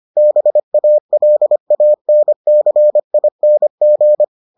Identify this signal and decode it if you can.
Morse code, heard as a keyed tone, about 25 wpm, 600 Hz. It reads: BALANCING